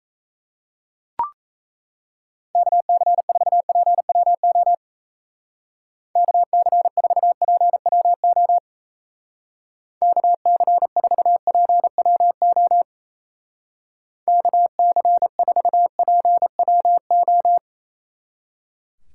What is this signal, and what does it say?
Morse code, heard as a keyed tone, about 35 wpm, 700 Hz. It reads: KC4PWO KC4PWO KC4PWO KC4PWO
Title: Speed Racing - USA 2x3 Call Signs 28wpm